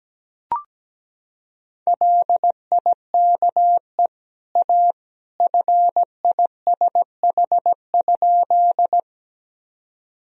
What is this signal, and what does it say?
Morse code, heard as a keyed tone, about 17 wpm, 700 Hz. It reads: LIKE A FISH?
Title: Sentences from Top 1000 Words - Review Part 1 - 17wpm